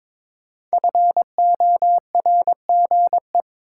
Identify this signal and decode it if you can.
Morse code, heard as a keyed tone, about 23 wpm, 700 Hz. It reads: FORGE
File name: 5 Letter Words - First 3 Letters Given - 22wpm